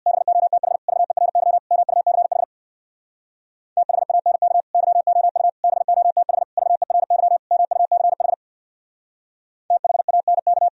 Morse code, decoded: BOTH VERY MUCH THANK YOU BOTH VERY MUCH THANK

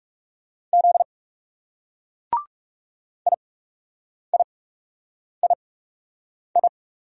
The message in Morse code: Z IIII

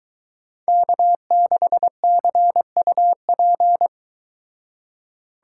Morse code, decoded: K6CUP

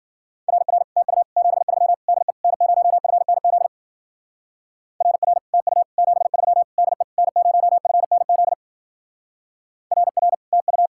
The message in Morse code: RR TU 73 DE N0FNZ RR TU 73 DE N0FNZ RR TU

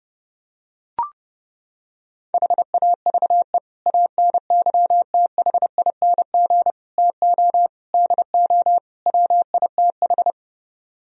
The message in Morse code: HAVE ANYTHING TO DO WITH

